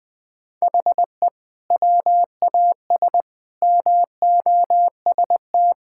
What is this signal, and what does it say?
Morse code, heard as a keyed tone, about 20 wpm, 700 Hz. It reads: HE WAS MOST
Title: Sentences from Top 1000 Words - Review Part 6 - 20wpm